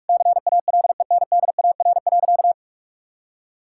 Morse code, heard as a keyed tone, about 41 wpm, 700 Hz. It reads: CALENDAR.
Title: Sentences from Top 1600 Words 45wpm